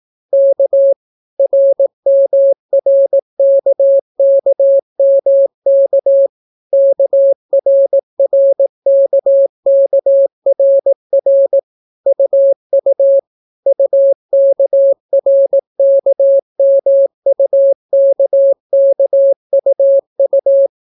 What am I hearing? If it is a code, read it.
Morse code, heard as a keyed tone, about 18 wpm, 550 Hz. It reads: K RMRKKMK KRRKKRR UU UKRKMUKKUU